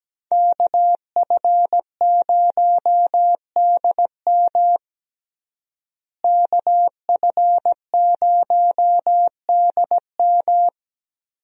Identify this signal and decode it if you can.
Morse code, heard as a keyed tone, about 17 wpm, 700 Hz. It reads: KF0DM KF0DM
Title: USA Call Signs - 5 Characters - Repeated 2x - 17wpm